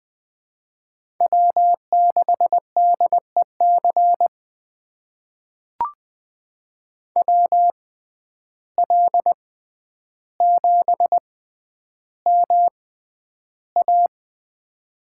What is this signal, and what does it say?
Morse code, heard as a keyed tone, about 20 wpm, spaced as for 10 wpm, 700 Hz. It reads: W6DEC WL7MA
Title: USA Call Signs - 5 Characters - 6x Character Spacing - 20wpm